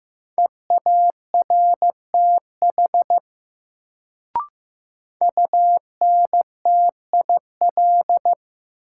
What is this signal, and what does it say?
Morse code, heard as a keyed tone, about 15 wpm, 700 Hz. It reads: EARTH UNTIL
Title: Sets of 5 Words Encore - New Word Drill - 15wpm